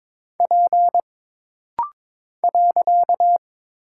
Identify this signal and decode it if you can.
Morse code, heard as a keyed tone, about 22 wpm, 700 Hz. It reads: P .
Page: This rapid-fire timing is geared to push your ICR (Instant Character Recognition) skills.